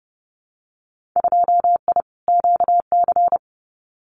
Morse code, heard as a keyed tone, about 30 wpm, 700 Hz. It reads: 2I QC
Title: Winter Field Day Exchanges 30wpm